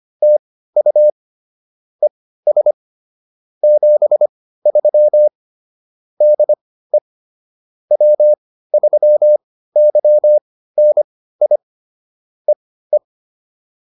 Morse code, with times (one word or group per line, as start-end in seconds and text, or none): 0.2-1.1 TU
2.0-2.7 ES
3.6-5.3 73
6.2-7.0 DE
7.9-11.6 W3YNI
12.5-13.0 EE